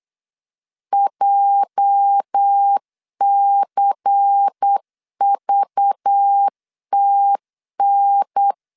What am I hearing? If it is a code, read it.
Morse code, heard as a keyed tone, about 8 wpm, 800 Hz. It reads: JCVTN
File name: LEZIONI / 8 YJ45 / 8 YJ 8wpm 3313